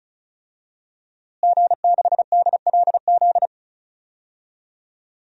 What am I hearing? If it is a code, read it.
Morse code, heard as a keyed tone, about 34 wpm, 700 Hz. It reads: G6DLZ
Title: DXCC-100 Call Signs - 5 Characters - Repeated 3x - 35wpm